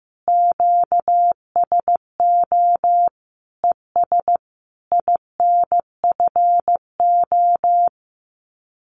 Morse code, heard as a keyed tone, about 15 wpm, 700 Hz. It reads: QSO ES INFO
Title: Common QSO Phrases Encore 15wpm